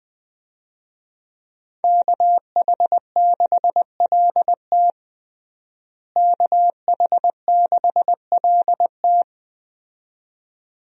Morse code, heard as a keyed tone, about 20 wpm, 700 Hz. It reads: KH6LT KH6LT
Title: USA Call Signs - 5 Characters - Repeated 2x - 20wpm